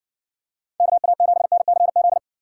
Morse code, heard as a keyed tone, about 58 wpm, 700 Hz. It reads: KA7NCZ